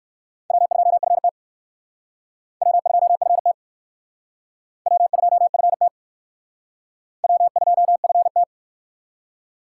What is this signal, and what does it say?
Morse code, heard as a keyed tone, about 58 wpm, 700 Hz. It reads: W2FT W2FT W2FT W2FT